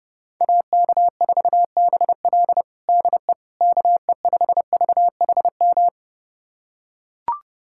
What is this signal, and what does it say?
Morse code, heard as a keyed tone, about 32 wpm, 700 Hz. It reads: AK4BL DE KE5VHM
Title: USA Callsign DE Callsign 30wpm